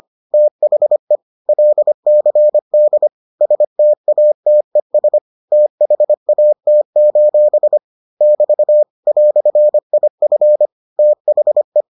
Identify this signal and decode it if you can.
Morse code, heard as a keyed tone, about 25 wpm, 600 Hz. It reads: THE LCD STATES THAT: <BT> "IF THE